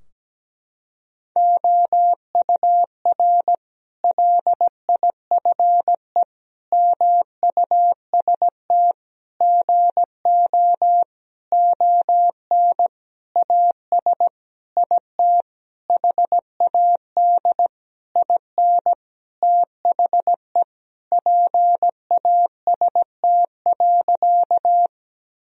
Morse code, heard as a keyed tone, about 17 wpm, 700 Hz. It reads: OUR LIFE MUST GO ON AS IT HAD IN THE PAST.
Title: Sentences from Top 1000 Words - Review Part 3 - 17wpm